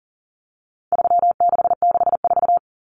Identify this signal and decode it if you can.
Morse code, heard as a keyed tone, about 40 wpm, 700 Hz. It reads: 3664
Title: Common Cut Numbers - 4-Digit 40wpm